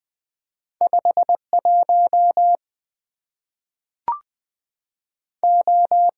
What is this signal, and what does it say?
Morse code, heard as a keyed tone, about 20 wpm, 700 Hz. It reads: 5 1 O
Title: Two Letter-Number Combinations - 20wpm